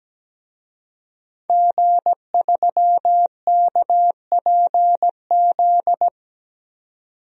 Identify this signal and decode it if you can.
Morse code, heard as a keyed tone, about 17 wpm, 700 Hz. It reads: G3KPZ
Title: DXCC-100 Call Signs - 5 Characters - 17wpm